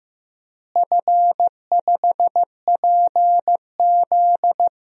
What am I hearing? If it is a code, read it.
Morse code, heard as a keyed tone, about 15 wpm, 700 Hz. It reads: F5PZ
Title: DXCC-100 Call Signs - 4 Characters - Repeated 3x - 15wpm